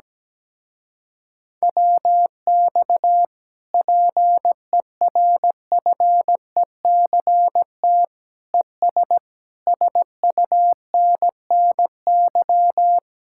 Morse code, heard as a keyed tone, about 17 wpm, 700 Hz. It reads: WX PERFECT ES SUNNY
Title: Speed-Racing Common QSO Phrases - 17wpm